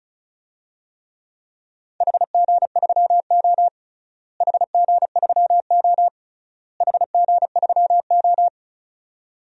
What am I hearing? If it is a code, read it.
Morse code, heard as a keyed tone, about 36 wpm, 700 Hz. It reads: HG3O HG3O HG3O
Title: DXCC-100 Call Signs - 4 Characters - Repeated 3x - 35wpm